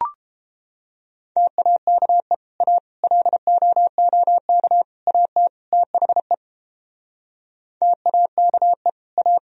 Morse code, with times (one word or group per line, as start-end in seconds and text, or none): 1.4-2.3 TAKE
2.6-2.8 A
3.0-4.8 LOOK
5.1-5.5 AT
5.7-6.3 THE
7.8-8.9 TAKE
9.2-9.4 A